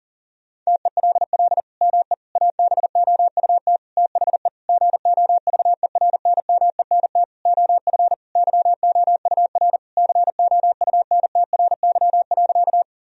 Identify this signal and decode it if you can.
Morse code, heard as a keyed tone, about 40 wpm, 700 Hz. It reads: TELL ME ABOUT THE GOVERNMENT OF YOUR COUNTRY.
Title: Sentences from Top 1000 Words - Review Part 3 - 40wpm